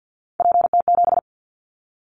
Morse code, heard as a keyed tone, about 50 wpm, 700 Hz. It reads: PT7
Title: Callsign Prefixes for Top 50 DXCC - 50wpm